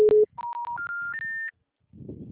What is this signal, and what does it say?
Special information tone.